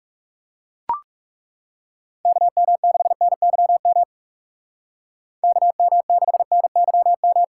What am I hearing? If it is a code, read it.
Morse code, heard as a keyed tone, about 38 wpm, 700 Hz. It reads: KM6NYM KM6NYM
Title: Speed Racing - USA 2x3 Call Signs 30wpm